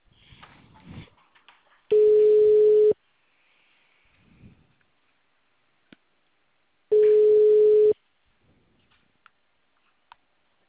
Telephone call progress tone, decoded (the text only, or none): ringback tone